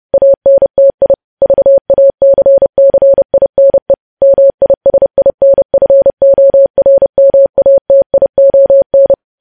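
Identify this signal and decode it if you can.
Morse code, heard as a keyed tone, about 30 wpm, 550 Hz. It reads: ANTI VACCINE MISINFORMATION
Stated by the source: Morse code practice of news headlines at 30 wpm, preceded by vvv.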